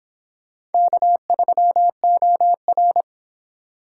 Morse code, heard as a keyed tone, about 29 wpm, 700 Hz. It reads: K3OR